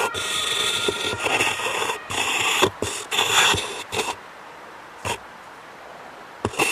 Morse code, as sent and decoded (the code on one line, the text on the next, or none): none